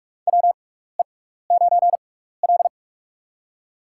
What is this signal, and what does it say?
Morse code, heard as a keyed tone, about 42 wpm, spaced as for 15 wpm, 700 Hz. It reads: WE9L
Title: USA Call Signs - 6x Character Spacing - 45wpm